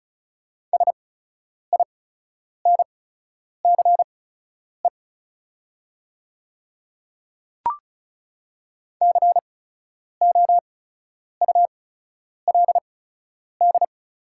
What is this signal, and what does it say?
Morse code, heard as a keyed tone, about 36 wpm, spaced as for 10 wpm, 700 Hz. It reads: SINCE COULD
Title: 8x Character Spacing - Top 300 Words - 35wpm